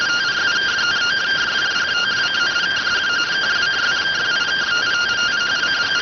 Two-tone teleprinter signal, about 170 Hz shift.
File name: findings / CT 205 ax mips, iDose 3 0004.gen-32 rx